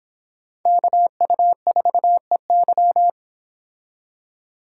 Morse code, heard as a keyed tone, about 31 wpm, 700 Hz. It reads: KU4EY